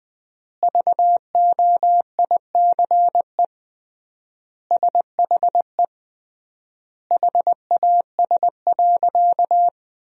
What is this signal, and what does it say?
Morse code, heard as a keyed tone, about 21 wpm, 700 Hz. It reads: VOICE SHE HAS.
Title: 3x Word-Spacing - Sentences from Top 500 Words - 20wpm